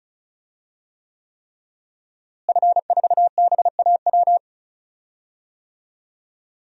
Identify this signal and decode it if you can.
Morse code, heard as a keyed tone, about 34 wpm, 700 Hz. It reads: F4BAW
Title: DXCC-100 Call Signs - 5 Characters - Repeated 3x - 35wpm